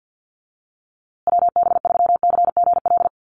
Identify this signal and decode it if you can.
Morse code, heard as a keyed tone, about 48 wpm, 700 Hz. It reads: WB3CGL
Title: USA Call Signs Repeated 2x 50wpm